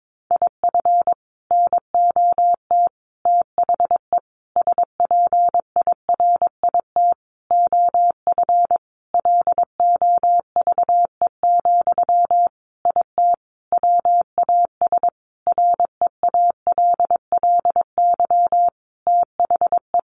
Morse code, heard as a keyed tone, about 22 wpm, 700 Hz. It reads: IF NOT THE SPIRIT OF LOVE, IT WAS REALLY THE
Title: Sentences from Top 1400 Words 22wpm